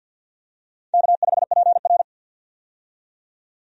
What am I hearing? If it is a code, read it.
Morse code, heard as a keyed tone, about 53 wpm, 700 Hz. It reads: K5PR
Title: USA Call Signs - 4 Characters - 50wpm